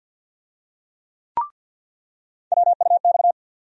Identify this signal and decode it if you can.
Morse code, heard as a keyed tone, about 48 wpm, 700 Hz. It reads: WUX
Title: USA Call Sign Postfixes - 3 Letters 50wpm